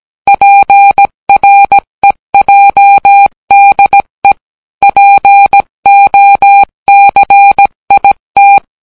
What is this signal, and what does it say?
Morse code, heard as a keyed tone, about 17 wpm, 800 Hz. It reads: PREJDE POCIT